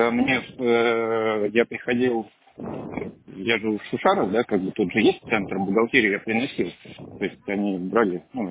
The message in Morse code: none

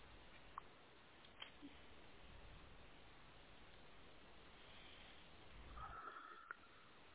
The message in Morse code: none

